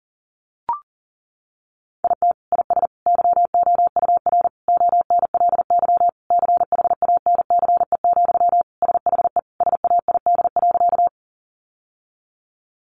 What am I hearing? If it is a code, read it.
Morse code, heard as a keyed tone, about 40 wpm, 700 Hz. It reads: IT IS YOUR ONLY CHANCE, SHE SAID.
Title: Sentences from Top 800 Words 40wpm